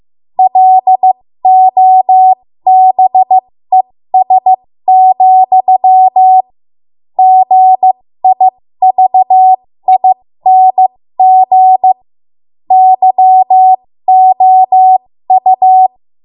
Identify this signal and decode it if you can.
Morse code, heard as a keyed tone, about 15 wpm, spaced as for 13 wpm, 750 Hz. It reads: LOBES, GIVING YOU
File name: Morse / 13WPM